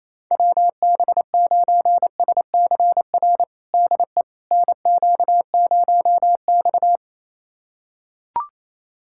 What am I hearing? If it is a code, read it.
Morse code, heard as a keyed tone, about 28 wpm, 700 Hz. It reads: WB9SCR DE NQ0X
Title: USA Callsign DE Callsign 28wpm